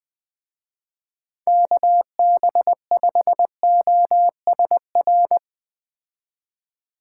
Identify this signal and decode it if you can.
Morse code, heard as a keyed tone, about 21 wpm, 700 Hz. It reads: KB5OSR